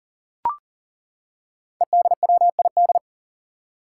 Morse code, heard as a keyed tone, about 37 wpm, 700 Hz. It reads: EDWID
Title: Five Challenging Letter-Number Combinations 40wpm